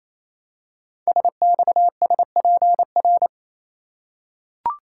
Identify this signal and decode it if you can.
Morse code, heard as a keyed tone, about 29 wpm, 700 Hz. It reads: SXSPR
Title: Code Groups - 5 Letters - 28wpm